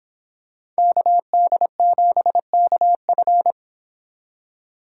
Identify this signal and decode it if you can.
Morse code, heard as a keyed tone, about 25 wpm, 700 Hz. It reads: KD7KF